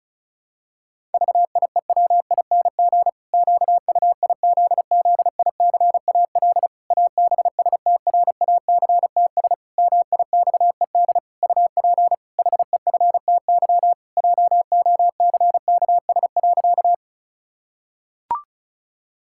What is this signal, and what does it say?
Morse code, heard as a keyed tone, about 36 wpm, 700 Hz. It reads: VIEWING QUIZZICAL ABSTRACTS MIXED UP HEFTY JOCKS.
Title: Pangram Challenge v2 - 35wpm